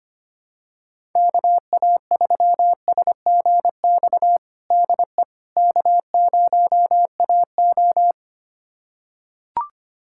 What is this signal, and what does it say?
Morse code, heard as a keyed tone, about 25 wpm, 700 Hz. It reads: KA3SGX DE K0AO
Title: USA Callsign DE Callsign 25wpm